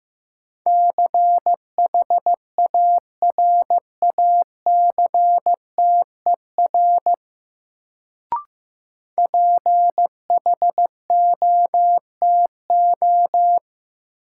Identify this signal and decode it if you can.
Morse code, heard as a keyed tone, about 15 wpm, 700 Hz. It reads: CHARACTER PHOTO
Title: Sentences from Top 1200 Words - New Word Drill - 15wpm